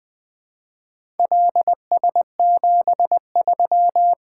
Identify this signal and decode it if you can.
Morse code, heard as a keyed tone, about 20 wpm, 700 Hz. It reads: LS73
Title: Four Letter-Number Combinations - 20wpm